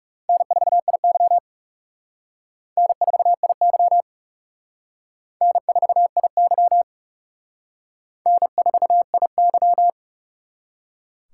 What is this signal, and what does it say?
Morse code, heard as a keyed tone, about 36 wpm, 700 Hz. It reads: N4IY N4IY N4IY N4IY
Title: Speed Racing - USA 1x2 Call Signs 30wpm